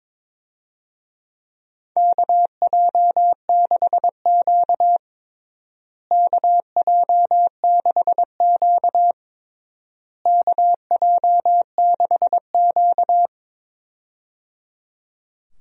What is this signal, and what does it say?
Morse code, heard as a keyed tone, about 22 wpm, 700 Hz. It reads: KJ6Q KJ6Q KJ6Q
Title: USA Call Signs - 4 Characters - Repeated 3x - 22wpm